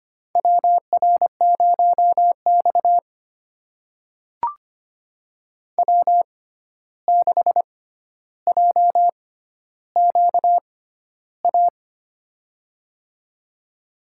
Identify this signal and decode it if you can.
Morse code, heard as a keyed tone, about 25 wpm, spaced as for 12 wpm, 700 Hz. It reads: WR0X W6JQA
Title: USA Call Signs - 6x Character Spacing - 25wpm